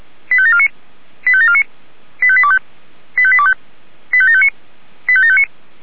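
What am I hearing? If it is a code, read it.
Five-tone selective call (ZVEI), heard as quick runs of five tones, tones of about 70 ms. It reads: 85749 85749 85825 85825 85759 85759
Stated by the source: ZVEI five-tone selective call received on an SDR.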